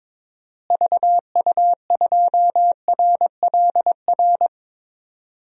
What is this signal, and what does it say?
Morse code, heard as a keyed tone, about 22 wpm, 700 Hz. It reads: VU2RLR
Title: DXCC-100 Call Signs - 6 Characters - Repeated 3x - 22wpm